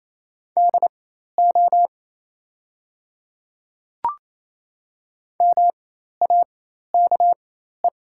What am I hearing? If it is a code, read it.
Morse code, heard as a keyed tone, about 27 wpm, spaced as for 13 wpm, 690 Hz. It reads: DO MAKE
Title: 4x Character Spacing - Top 100 Words - 28wpm